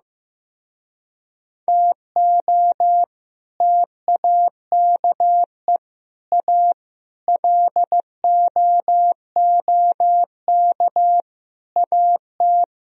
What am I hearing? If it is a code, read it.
Morse code, heard as a keyed tone, about 16 wpm, 700 Hz. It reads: TO TAKE A LOOK AT
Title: Speed-Racing - Sets of 5 Words 15wpm